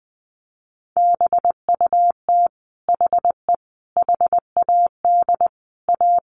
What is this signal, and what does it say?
Morse code, heard as a keyed tone, about 20 wpm, 700 Hz. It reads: BUT HE HAD A
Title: Sentences from Top 1200 Words 20wpm